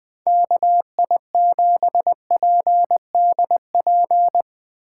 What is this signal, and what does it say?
Morse code, heard as a keyed tone, about 21 wpm, 700 Hz. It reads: KI7PDP